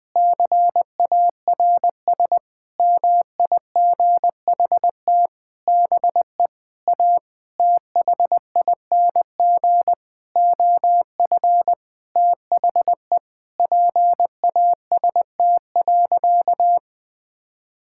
Morse code, heard as a keyed tone, about 20 wpm, 700 Hz. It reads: CARS MIGHT BE A THING OF THE PAST.
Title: Sentences with 6-Word Phrases - Spoken Beginning - 20wpm